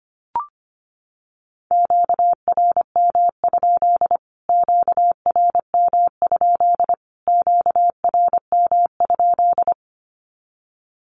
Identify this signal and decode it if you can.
Morse code, heard as a keyed tone, about 25 wpm, 700 Hz. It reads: QRM? QRM? QRM?
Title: Q-Codes Repeated 3x 25wpm